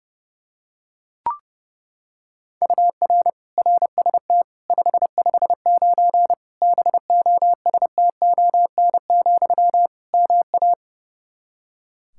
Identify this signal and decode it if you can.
Morse code, heard as a keyed tone, about 32 wpm, 700 Hz. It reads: UR RST 559 BOSTON, MA
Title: Practice QSO - Exchange 1 - Beginning - 30wpm